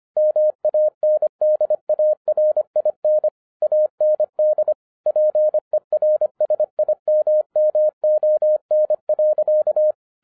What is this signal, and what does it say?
Morse code, heard as a keyed tone, about 25 wpm, 610 Hz. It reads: MANDARIN AND PERSIMMON.
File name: 2011-10-mp3 / n04-JUST MOVED 25wpm